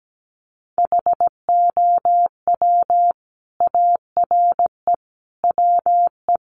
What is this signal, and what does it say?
Morse code, heard as a keyed tone, about 17 wpm, 700 Hz. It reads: HOW ARE WE
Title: Sentences from Top 1300 Words 17wpm